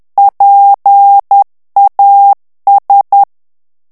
Morse code, heard as a keyed tone, about 10 wpm, 800 Hz. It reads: PAS
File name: morse mp3 / FRANCAIS TEXTE CLAIR 2 VITESSE 10 MOTS MINUTES